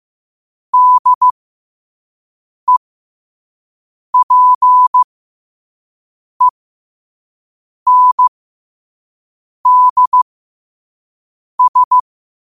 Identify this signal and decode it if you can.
Morse code, heard as a keyed tone, about 15 wpm, spaced as for 5 wpm, 1000 Hz. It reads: DEPENDS
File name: qotdfiles / 5wpm